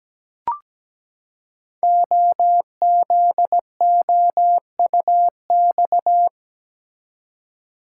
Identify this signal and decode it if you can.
Morse code, heard as a keyed tone, about 17 wpm, 700 Hz. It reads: OZOUX
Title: Code Groups - 5 Letter:Numbers - 17wpm